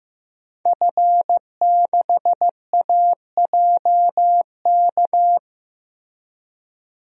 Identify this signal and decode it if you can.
Morse code, heard as a keyed tone, about 15 wpm, 700 Hz. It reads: F6AJK
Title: DXCC-100 Call Signs - 5 Characters - Repeated 3x - 15wpm